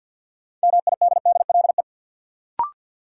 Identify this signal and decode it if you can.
Morse code, heard as a keyed tone, about 48 wpm, 700 Hz. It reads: MIDDLE
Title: Sentences from Top 900 Words - New Word Drill - 50wpm